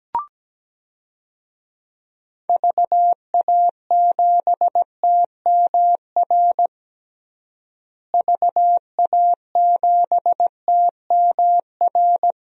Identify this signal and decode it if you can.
Morse code, heard as a keyed tone, about 17 wpm, 700 Hz. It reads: VA7TMR VA7TMR
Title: DXCC-100 Call Signs - 6 Characters - Repeated 3x - 17wpm